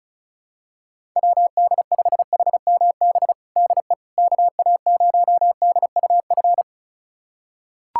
Morse code, heard as a keyed tone, about 36 wpm, 700 Hz. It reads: WD5HMB DE KA0DUF
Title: USA Callsign DE Callsign 35wpm